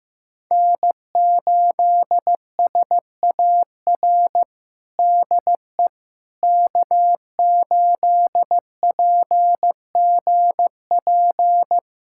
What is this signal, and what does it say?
Morse code, heard as a keyed tone, about 15 wpm, 700 Hz. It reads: N8SAR DE K8PGP
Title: USA Callsign DE Callsign - 5 Characters - 15wpm